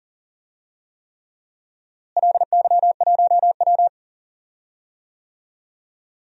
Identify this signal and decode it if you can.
Morse code, heard as a keyed tone, about 40 wpm, 700 Hz. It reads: LY1W
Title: DXCC-100 Call Signs - 4 Characters - Repeated 3x - 40wpm